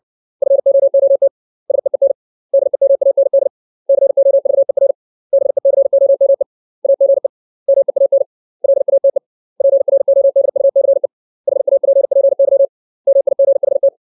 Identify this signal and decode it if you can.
Morse code, heard as a keyed tone, about 62 wpm, 550 Hz. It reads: FOOT SEEN DEMAND COVER BECOME AGE MEAN LATE PRODUCE HAPPY MIGHT
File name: 632 words / 632 Words 60WPM